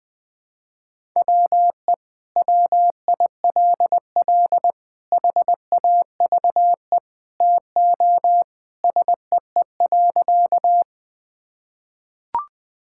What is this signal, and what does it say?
Morse code, heard as a keyed tone, about 19 wpm, 700 Hz. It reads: WE WILL HAVE TO SEE.
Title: Sentences with 3-Word Phrases - 20wpm